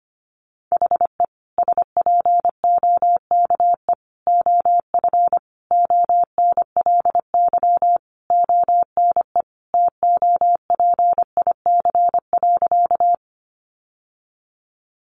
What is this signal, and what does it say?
Morse code, heard as a keyed tone, about 25 wpm, 700 Hz. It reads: HE SPOKE OF ONLY ONE TOPIC.
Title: Sentences from Top 1200 Words 25wpm